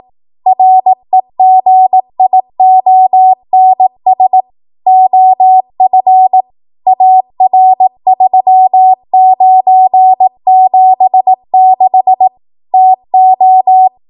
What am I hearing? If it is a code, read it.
Morse code, heard as a keyed tone, about 18 wpm, 750 Hz. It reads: REGIONS OF AR3976 TO
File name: Morse / 18WPM